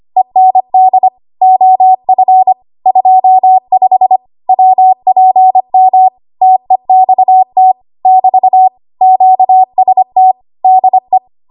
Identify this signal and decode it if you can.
Morse code, heard as a keyed tone, about 25 wpm, 750 Hz. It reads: END OF 25 WPM TEXT <BT> QST DE
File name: Morse / 25WPM